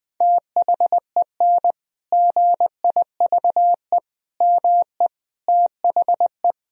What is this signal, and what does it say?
Morse code, heard as a keyed tone, about 20 wpm, 700 Hz. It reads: THEN GIVE ME THE
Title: Sentences from Top 1000 Words - Review Part 1 - 20wpm